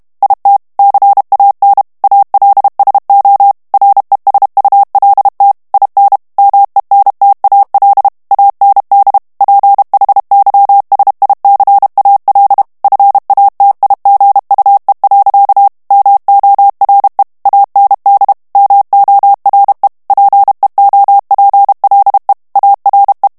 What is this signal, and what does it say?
Morse code, heard as a keyed tone, about 32 wpm, 800 Hz. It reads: IT CAN ALSO RESULT IN MENTAL AND PHYSICAL FATIGUE. MORE AND MORE PEOPLE ARE
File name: morse mp3 / ANGLAIS texte clair 1 vitesse 30 mots minute